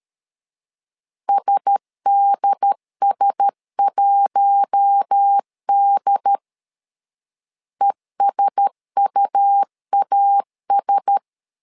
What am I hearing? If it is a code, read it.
Morse code, heard as a keyed tone, about 13 wpm, 800 Hz. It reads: SDS1D ESUAS
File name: LEZIONI / 4 Q1S 12wpm 1802